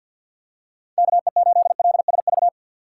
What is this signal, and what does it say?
Morse code, heard as a keyed tone, about 53 wpm, 700 Hz. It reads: KE8LSV
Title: USA Call Signs - 6 Characters - 6x Character Spacing - 50wpm